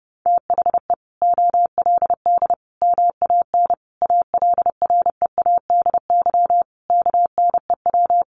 Morse code, heard as a keyed tone, about 28 wpm, 700 Hz. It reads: THE OLD MAN ALREADY KNEW
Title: Sentences from Top 700 Words 30wpm